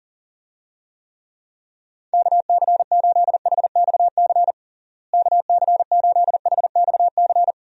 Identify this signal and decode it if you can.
Morse code, heard as a keyed tone, about 40 wpm, 700 Hz. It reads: KC8HXC KC8HXC
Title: USA Call Signs - 6 Characters - Repeated 2x - 40wpm